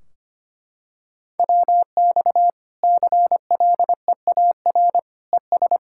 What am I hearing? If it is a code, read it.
Morse code, heard as a keyed tone, about 25 wpm, 700 Hz. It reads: WX CLEAR ES
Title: K1USN SST QSO Practice 25wpm